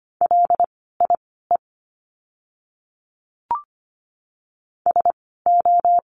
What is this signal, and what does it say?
Morse code, heard as a keyed tone, about 25 wpm, spaced as for 16 wpm, 700 Hz. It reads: LIE SO
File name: course / 25:10wpm / 094 Lesson 16f - Letter c - Words 25wpm